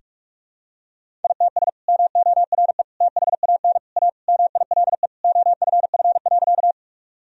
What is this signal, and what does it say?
Morse code, heard as a keyed tone, about 42 wpm, 700 Hz. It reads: ITS MORE THAN A MILE OFF.
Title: Sentences from Top 1000 Words - Review Part 1 - 45wpm